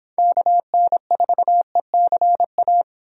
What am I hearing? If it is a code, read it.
Morse code, heard as a keyed tone, about 30 wpm, 700 Hz. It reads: KN4ECA